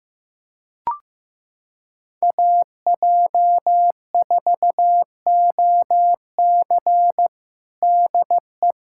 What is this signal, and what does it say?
Morse code, heard as a keyed tone, about 15 wpm, 700 Hz. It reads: AJ4OC DE
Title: Practice QSO - Exchange 4 - Ending - 15wpm